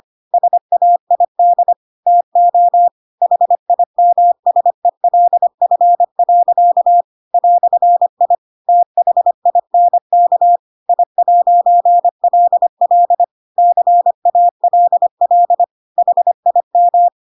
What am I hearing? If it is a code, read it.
Morse code, heard as a keyed tone, about 25 wpm, 700 Hz. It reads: SAID TO HIMSELF. "I THINK I'LL CALL HIM
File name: Pinocchio / Pinocchio ch3 25 25 ik1hge